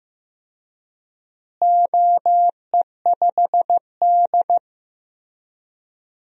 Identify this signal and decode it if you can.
Morse code, heard as a keyed tone, about 15 wpm, 700 Hz. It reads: OE5D
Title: DXCC-100 Call Signs - 4 Characters - Repeated 3x - 15wpm